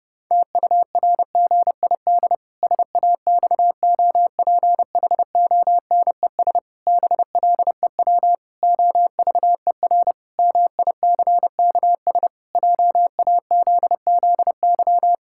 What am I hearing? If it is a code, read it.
Morse code, heard as a keyed tone, about 30 wpm, 700 Hz. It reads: TURGID SAXOPHONES BLEW OVER MICKS JAZZY
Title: Pangram Challenge v2 - 30wpm